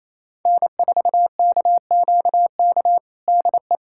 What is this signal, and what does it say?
Morse code, heard as a keyed tone, about 28 wpm, 700 Hz. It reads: N4KQK DE
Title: USA Callsign DE Callsign - 5 Characters - 28wpm